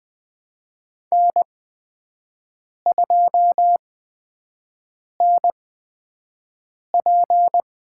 Morse code, heard as a keyed tone, about 20 wpm, spaced as for 5 wpm, 700 Hz. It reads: N2NP
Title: USA Call Signs - 4 Characters - 8x Character Spacing - 20wpm